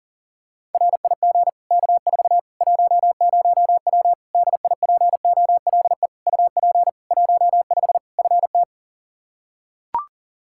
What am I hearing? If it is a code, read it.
Morse code, heard as a keyed tone, about 40 wpm, 700 Hz. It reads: RIG K4 10W DIPOLE UP 15 FT
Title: Practice QSO - Exchange 2 - Info - 40wpm